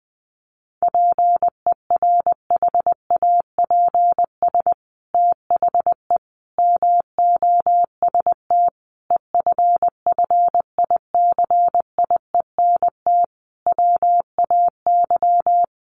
Morse code, heard as a keyed tone, about 20 wpm, 700 Hz. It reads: PERHAPS THE MOST EFFICIENT WAY
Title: Sentences from Top 1300 Words 20wpm